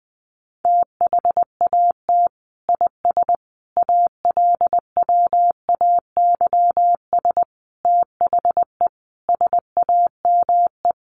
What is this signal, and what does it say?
Morse code, heard as a keyed tone, about 20 wpm, 700 Hz. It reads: THAT IS ALWAYS THE SAME
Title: Sentences from Top 700 Words 20wpm